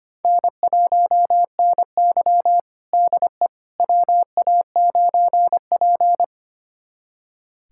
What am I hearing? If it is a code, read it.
Morse code, heard as a keyed tone, about 25 wpm, 700 Hz. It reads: N1NY DE WA9P
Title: USA Callsign DE Callsign - 4 Characters - 25wpm